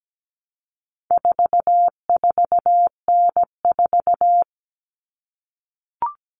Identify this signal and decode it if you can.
Morse code, heard as a keyed tone, about 17 wpm, 700 Hz. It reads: 44N4
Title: All Cut Numbers - 4-Digit 17wpm